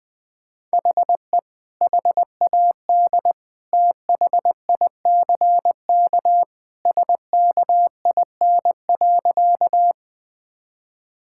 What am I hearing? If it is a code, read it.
Morse code, heard as a keyed tone, about 20 wpm, 700 Hz. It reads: HE HAD THICK SKIN.
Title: Sentences from Top 1600 Words 20wpm